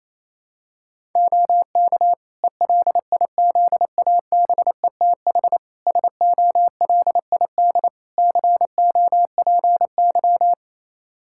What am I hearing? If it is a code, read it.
Morse code, heard as a keyed tone, about 28 wpm, 700 Hz. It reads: OK ELIZABETH SOLID COPY
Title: Practice QSO - Exchange 3 - More Info - 28wpm